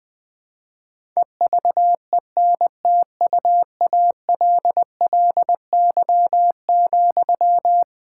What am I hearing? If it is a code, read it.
Morse code, heard as a keyed tone, about 20 wpm, 700 Hz. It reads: EVENTUALLY,
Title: Sentences from Top 1000 Words - Review Part 6 - 20wpm